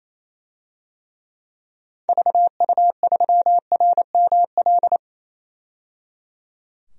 Morse code, heard as a keyed tone, about 28 wpm, 700 Hz. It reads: VU3RML
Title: DXCC-100 Call Signs - 6 Characters - 28wpm